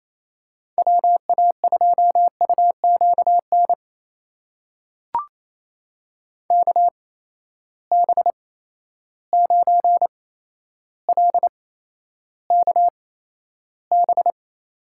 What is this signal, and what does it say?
Morse code, heard as a keyed tone, about 28 wpm, spaced as for 12 wpm, 700 Hz. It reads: WA2UQN KB9LKB
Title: USA Call Signs - 6 Characters - 8x Character Spacing - 28wpm